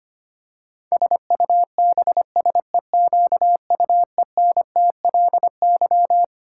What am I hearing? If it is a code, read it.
Morse code, heard as a keyed tone, about 25 wpm, 700 Hz. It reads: SUBSEQUENTLY
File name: practice / 25wpm / Twelve Letter Words - 25wpm